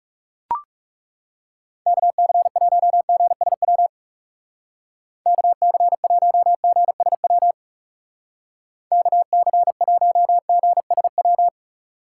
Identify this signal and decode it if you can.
Morse code, heard as a keyed tone, about 38 wpm, 700 Hz. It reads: KC1GSW KC1GSW KC1GSW